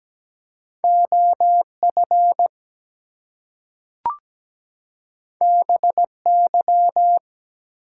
Morse code, heard as a keyed tone, about 17 wpm, 700 Hz. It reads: OF BY
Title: Two Letter Words - 17wpm